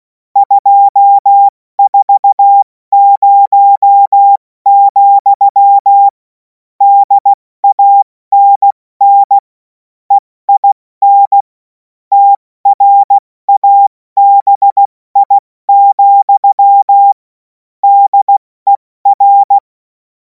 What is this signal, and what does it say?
Morse code, heard as a keyed tone, about 16 wpm, 800 Hz. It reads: 240, DANN EIN TRABI, DER
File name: Morsekurs / Witze / Witz 05. Mai 2026 70 F80